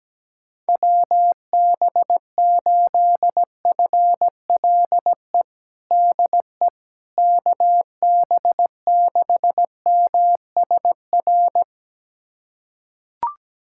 Morse code, heard as a keyed tone, about 17 wpm, 700 Hz. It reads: WB8FLE DE KB6MSR
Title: USA Callsign DE Callsign - 6 Characters - 17wpm